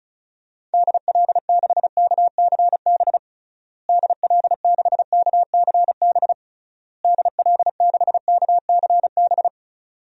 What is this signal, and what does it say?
Morse code, heard as a keyed tone, about 36 wpm, 700 Hz. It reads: DL6KCB DL6KCB DL6KCB
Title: DXCC-100 Call Signs - 6 Characters - Repeated 3x - 35wpm